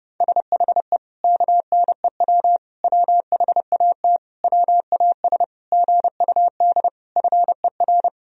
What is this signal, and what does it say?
Morse code, heard as a keyed tone, about 30 wpm, 700 Hz. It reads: SHE KNEW WHAT WAS GUD FER
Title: Sentences from Top 400 Words and Abbreviations 30wpm